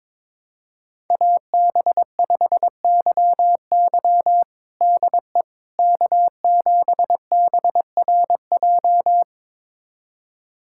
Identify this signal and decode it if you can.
Morse code, heard as a keyed tone, about 22 wpm, 700 Hz. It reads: AB5YY DE K7BRJ
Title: USA Callsign DE Callsign - 5 Characters - 22wpm